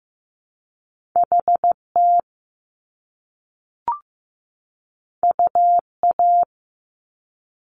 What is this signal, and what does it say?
Morse code, heard as a keyed tone, about 15 wpm, 700 Hz. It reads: HT UA
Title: Top Two Letter Combinations in Words 15wpm